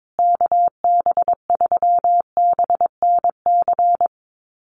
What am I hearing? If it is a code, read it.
Morse code, heard as a keyed tone, about 22 wpm, 700 Hz. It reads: KB3BNC
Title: USA Call Signs Repeated 3x 22wpm